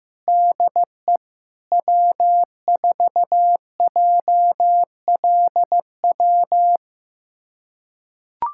Morse code, heard as a keyed tone, about 15 wpm, 700 Hz. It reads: DE W4JLW
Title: USA Callsign DE Callsign 15wpm